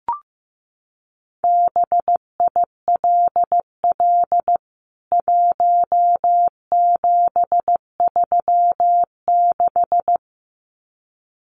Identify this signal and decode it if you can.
Morse code, heard as a keyed tone, about 15 wpm, 700 Hz. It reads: BILL 1736
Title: CWT Exchange 15wpm